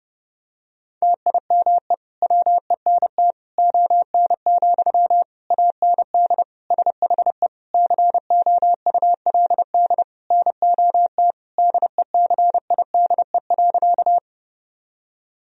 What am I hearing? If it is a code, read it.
Morse code, heard as a keyed tone, about 32 wpm, 700 Hz. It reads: TIME WENT ON, AND SHE COULD NOT DECIDE.
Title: Sentences from Top 1000 Words - Review Part 3 - 30wpm